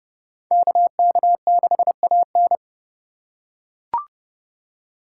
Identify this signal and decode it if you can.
Morse code, heard as a keyed tone, about 28 wpm, 700 Hz. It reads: KK6AN